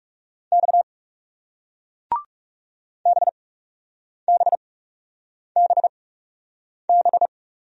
Morse code, heard as a keyed tone, about 39 wpm, spaced as for 8 wpm, 700 Hz. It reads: X BBBB